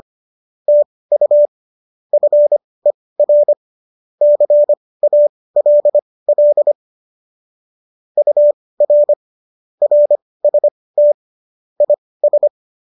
Morse code, heard as a keyed tone, about 25 wpm, spaced as for 18 wpm, 590 Hz. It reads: TU FER CALL UR RST IS